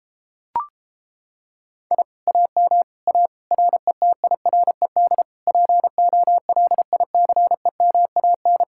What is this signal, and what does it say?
Morse code, heard as a keyed tone, about 29 wpm, 700 Hz. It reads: I AM A RETIRED POLICEMAN